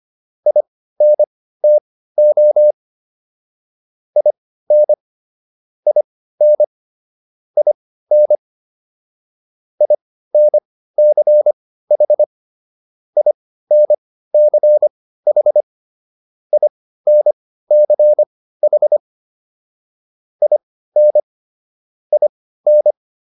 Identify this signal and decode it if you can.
Morse code, heard as a keyed tone, about 25 wpm, spaced as for 15 wpm, 600 Hz. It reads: INTO IN IN IN INCH INCH INCH IN IN